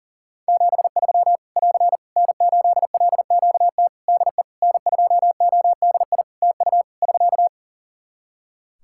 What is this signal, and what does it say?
Morse code, heard as a keyed tone, about 40 wpm, 700 Hz. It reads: 73 <AR> N8LQT DE N2ODI TU <SK>
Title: K1USN SST QSO Practice 40wpm